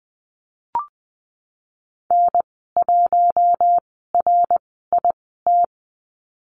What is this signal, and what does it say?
Morse code, heard as a keyed tone, about 20 wpm, spaced as for 15 wpm, 700 Hz. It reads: N1RIT
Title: 062 Lesson 11g - Letter h - Call Signs 20wpm